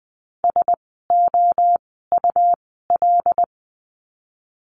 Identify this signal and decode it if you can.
Morse code, heard as a keyed tone, about 20 wpm, spaced as for 14 wpm, 700 Hz. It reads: SOUL
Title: 101 Lesson 17f - Letter u - Words 20wpm